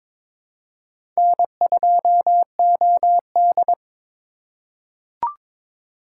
Morse code, heard as a keyed tone, about 23 wpm, 700 Hz. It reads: N2OD
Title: USA Call Signs - 4 Characters - 4x Character Spacing - 22wpm